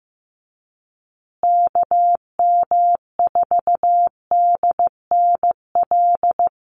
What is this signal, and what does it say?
Morse code, heard as a keyed tone, about 15 wpm, 700 Hz. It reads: KM4DNL
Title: USA Call Signs Repeated 3x 15wpm